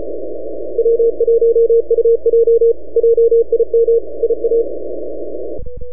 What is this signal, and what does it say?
Morse code, heard as a keyed tone, about 34 wpm, 470 Hz. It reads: W1UJ JIM IA
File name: NAQPCW-8-2009 / W1UJ NAQP CW 09 1800Z